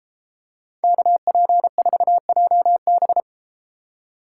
Morse code, heard as a keyed tone, about 31 wpm, 700 Hz. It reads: KP4JB